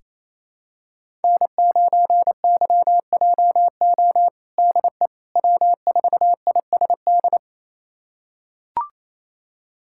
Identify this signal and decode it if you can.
Morse code, heard as a keyed tone, about 28 wpm, 700 Hz. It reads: N9YJO DE W4ISD
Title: USA Callsign DE Callsign - 5 Characters - 28wpm